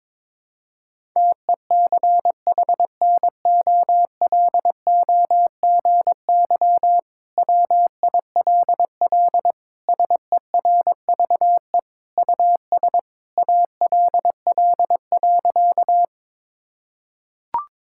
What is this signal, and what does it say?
Morse code, heard as a keyed tone, about 22 wpm, 700 Hz. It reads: TECHNOLOGY WILL SERVE US ALL.
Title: Sentences from Top 1000 Words - Review Part 5 - 22wpm